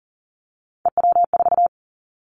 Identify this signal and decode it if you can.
Morse code, heard as a keyed tone, about 40 wpm, 700 Hz. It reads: EW4
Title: Callsign Prefixes for Top 100 DXCC - 40wpm